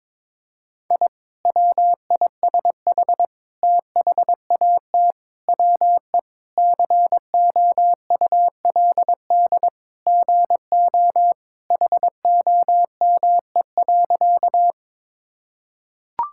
Morse code, heard as a keyed tone, about 22 wpm, 700 Hz. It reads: I WISH THAT WE COULD GO HOME.
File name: practice / 22wpm / Sentences with 2-Word Phrases - 22wpm